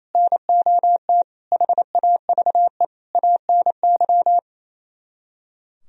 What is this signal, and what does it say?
Morse code, heard as a keyed tone, about 37 wpm, 700 Hz. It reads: NOT HAVE ANY